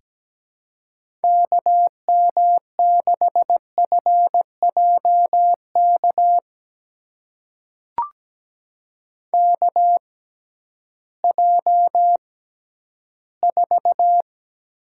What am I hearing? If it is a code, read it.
Morse code, heard as a keyed tone, about 17 wpm, spaced as for 11 wpm, 700 Hz. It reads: KM6FJK KJ4
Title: USA Call Signs - 6x Character Spacing - 17wpm